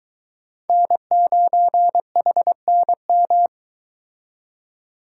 Morse code, heard as a keyed tone, about 23 wpm, 700 Hz. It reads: N9HNM